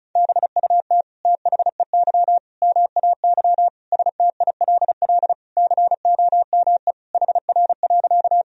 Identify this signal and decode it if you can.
Morse code, heard as a keyed tone, about 36 wpm, 700 Hz. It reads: BUT THEY MAY STILL COME HR.
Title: Sentences from Top 300 Words and Abbreviations 35wpm